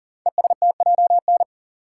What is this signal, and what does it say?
Morse code, heard as a keyed tone, about 39 wpm, 700 Hz. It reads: ESTJN
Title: Code Groups - 5 Letter:Numbers - 40wpm